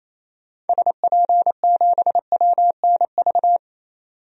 Morse code, heard as a keyed tone, about 27 wpm, 700 Hz. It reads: SP7WNV
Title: DXCC-100 Call Signs - 6 Characters - 28wpm